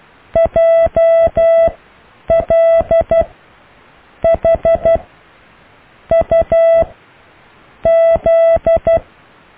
Morse code, heard as a keyed tone, about 12 wpm, spaced as for 6 wpm, 660 Hz. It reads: JLHUZ
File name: cw / Alle letters